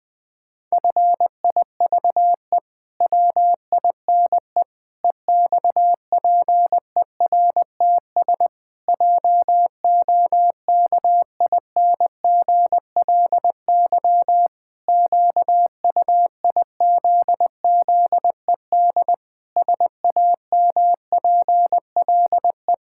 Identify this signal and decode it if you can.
Morse code, heard as a keyed tone, about 20 wpm, 700 Hz. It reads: FIVE WINE EXPERTS JOKINGLY QUIZZED SAMPLE
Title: Pangram Challenge v2 - 20wpm